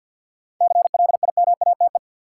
Morse code, heard as a keyed tone, about 52 wpm, 700 Hz. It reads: CLIMATE
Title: Sentences from Top 1600 Words - New Word Drill Alt - 50wpm